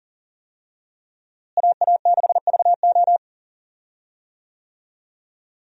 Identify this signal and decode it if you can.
Morse code, heard as a keyed tone, about 40 wpm, 700 Hz. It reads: AA6VO
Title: USA Call Signs - 5 Characters - Repeated 2x - 40wpm